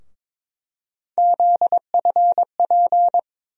Morse code, heard as a keyed tone, about 22 wpm, 700 Hz. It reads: ZFP
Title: USA Call Sign Postfixes - 3 Letters 22wpm